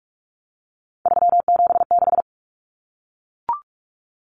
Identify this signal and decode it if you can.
Morse code, heard as a keyed tone, about 44 wpm, 700 Hz. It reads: 376
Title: Common Cut Numbers - 3-Digit 45wpm